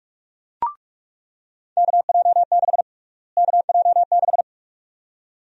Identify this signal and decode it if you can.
Morse code, heard as a keyed tone, about 46 wpm, 700 Hz. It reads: KJ6 KJ6
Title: USA Call Sign Prefixes - Repeated 2x - 45wpm